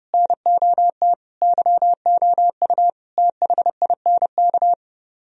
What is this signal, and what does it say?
Morse code, heard as a keyed tone, about 28 wpm, 700 Hz. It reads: NOT YOU THINK